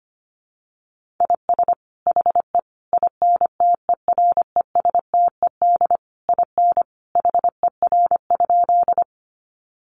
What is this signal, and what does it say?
Morse code, heard as a keyed tone, about 25 wpm, 700 Hz. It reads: IS HE INTERESTED IN HER?
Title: Sentences from Top 900 Words 25wpm